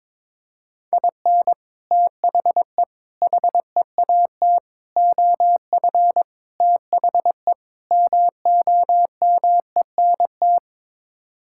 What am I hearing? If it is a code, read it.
Morse code, heard as a keyed tone, about 22 wpm, 700 Hz. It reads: IN THE HEAT OF THE MOMENT
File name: practice / 22wpm / Sets of 6 Words - 22wpm